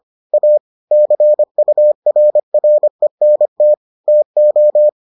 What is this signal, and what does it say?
Morse code, heard as a keyed tone, about 25 wpm, 600 Hz. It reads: A CURRENT TO